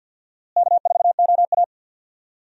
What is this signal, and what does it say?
Morse code, heard as a keyed tone, about 50 wpm, 700 Hz. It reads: K4OA
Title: USA Call Signs - 4 Characters - Repeated 2x - 50wpm